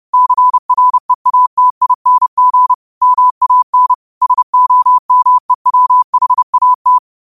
Morse code, heard as a keyed tone, about 30 wpm, 1000 Hz. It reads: CREATING MAN SOMEWHAT
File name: qotdfiles / 30wpm